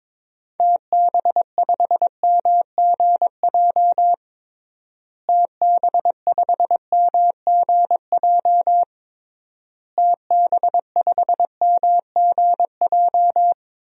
Morse code, heard as a keyed tone, about 22 wpm, 700 Hz. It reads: TB5MGJ TB5MGJ TB5MGJ
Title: DXCC-100 Call Signs - 6 Characters - Repeated 3x - 22wpm